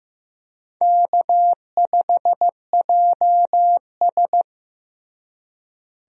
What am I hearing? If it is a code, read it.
Morse code, heard as a keyed tone, about 14 wpm, 700 Hz. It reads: K5JS
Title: USA Call Signs - 4 Characters - Repeated 3x - 15wpm